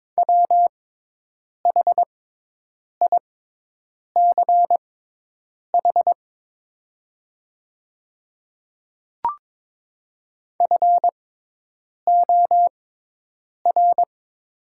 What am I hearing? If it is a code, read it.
Morse code, heard as a keyed tone, about 23 wpm, spaced as for 8 wpm, 700 Hz. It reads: WHICH FOR
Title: 6x Character Spacing - Top 200 Words - 22wpm